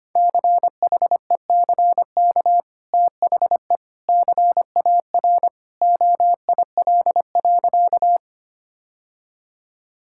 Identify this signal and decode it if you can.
Morse code, heard as a keyed tone, about 25 wpm, 700 Hz. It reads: CHECK THE CAR OIL.
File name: Easy Sentences with Hints - 25wpm